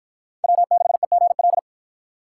Morse code, heard as a keyed tone, about 52 wpm, 700 Hz. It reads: W6EGL